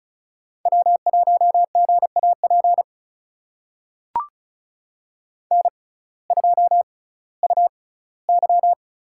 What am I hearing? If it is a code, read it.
Morse code, heard as a keyed tone, about 36 wpm, spaced as for 19 wpm, 700 Hz. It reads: W1GAP N2UY
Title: USA Call Signs - 5 Characters - 6x Character Spacing - 35wpm